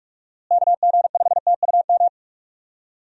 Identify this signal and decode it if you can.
Morse code, heard as a keyed tone, about 46 wpm, 700 Hz. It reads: KG5TUM
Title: USA Call Signs - 6 Characters - Repeated 3x - 45wpm